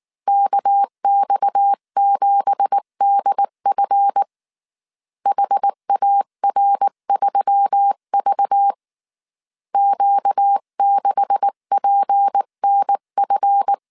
Morse code, heard as a keyed tone, about 19 wpm, 800 Hz. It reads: KX7DF HAR3V Q6PNF